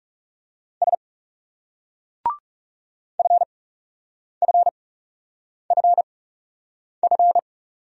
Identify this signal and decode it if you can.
Morse code, heard as a keyed tone, about 36 wpm, spaced as for 8 wpm, 700 Hz. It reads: S FFFF
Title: Speed-Racing Single Letters 30wpm